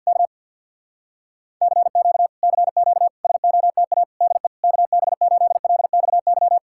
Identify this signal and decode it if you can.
Morse code, heard as a keyed tone, about 64 wpm, spaced as for 56 wpm, 700 Hz. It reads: K CQ CQ SOTA DE KB8LXY
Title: Speed-Racing Common QSO Phrases Encore - 50wpm